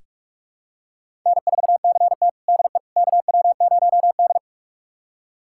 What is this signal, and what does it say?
Morse code, heard as a keyed tone, about 46 wpm, 700 Hz. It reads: N4CT DE KW0D
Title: USA Callsign DE Callsign - 4 Characters - 45wpm